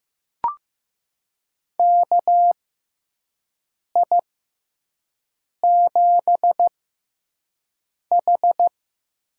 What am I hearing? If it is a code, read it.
Morse code, heard as a keyed tone, about 15 wpm, spaced as for 5 wpm, 700 Hz. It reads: KI7H
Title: USA Call Signs - 6 Characters - 6x Character Spacing - 15wpm